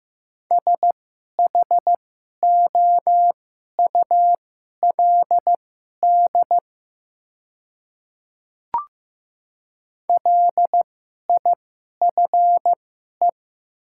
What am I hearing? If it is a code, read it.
Morse code, heard as a keyed tone, about 15 wpm, spaced as for 11 wpm, 700 Hz. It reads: SHOULD LIFE
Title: 2x Character Spacing - Top 300 Words - 15wpm